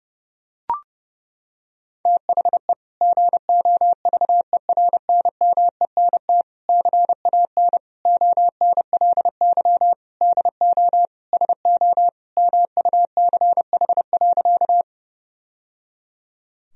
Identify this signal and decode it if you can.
Morse code, heard as a keyed tone, about 30 wpm, 700 Hz. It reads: THE GOVERNMENT CAN ONLY DO SO MUCH.
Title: Sentences from Top 1000 Words - Review Part 3 - 30wpm